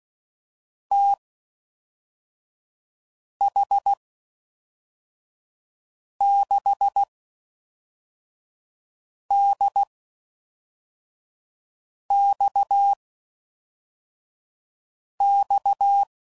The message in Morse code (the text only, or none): TH6DXX